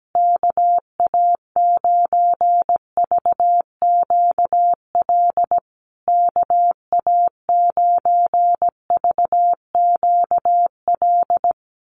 Morse code, heard as a keyed tone, about 17 wpm, 700 Hz. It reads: KA9VQL KA9VQL
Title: USA Call Signs Repeated 3x 17wpm